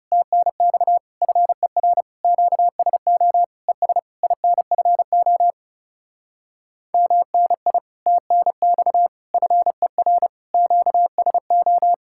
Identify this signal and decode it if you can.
Morse code, heard as a keyed tone, about 38 wpm, 700 Hz. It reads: TNX FER QSO ES INFO MNI TNX FER QSO